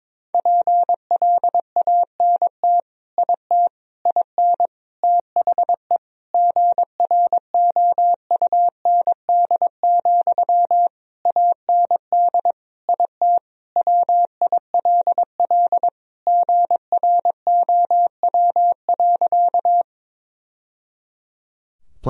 Morse code, heard as a keyed tone, about 22 wpm, 700 Hz. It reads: PLANT IT IN THE GROUND, AND IT WILL GROW.
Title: Sentences from Top 1000 Words - Review Part 2 - 22wpm